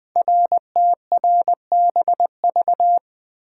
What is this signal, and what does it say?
Morse code, heard as a keyed tone, about 21 wpm, 700 Hz. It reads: RTRBV
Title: Code Groups - 5 Letters - 20wpm